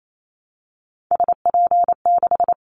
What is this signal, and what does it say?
Morse code, heard as a keyed tone, about 28 wpm, 700 Hz. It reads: SP6
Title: Callsign Prefixes for Top 25 DXCC - Repeated 3x - 28wpm